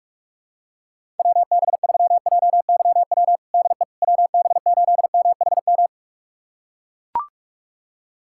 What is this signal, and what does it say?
Morse code, heard as a keyed tone, about 42 wpm, 700 Hz. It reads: WB3JYW DE WB8MHM
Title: USA Callsign DE Callsign - 6 Characters - 45wpm